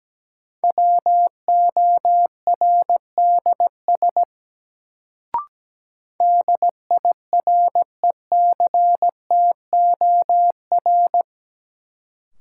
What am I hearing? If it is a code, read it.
Morse code, heard as a keyed tone, about 17 wpm, 700 Hz. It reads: WORDS DIRECTOR
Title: Sets of 3 Words - New Word Drill - 17wpm